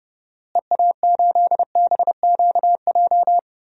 Morse code, heard as a keyed tone, about 31 wpm, 700 Hz. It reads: EA8BQJ
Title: DXCC-100 Call Signs - 6 Characters - Repeated 2x - 30wpm